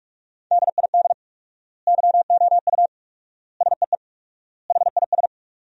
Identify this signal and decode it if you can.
Morse code, heard as a keyed tone, about 47 wpm, 700 Hz. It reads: DID YOU SEE HIS
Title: 4x Word-Spacing - Sentences from Top 300 Words - 45wpm